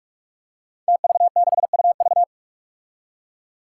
Morse code, heard as a keyed tone, about 44 wpm, 700 Hz. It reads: TV6UV
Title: Five Challenging Letter-Number Combinations 45wpm